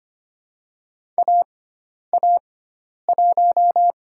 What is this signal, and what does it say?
Morse code, heard as a keyed tone, about 25 wpm, 700 Hz. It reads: A A 1
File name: Lesson 16g - Letter c - Call Signs 25wpm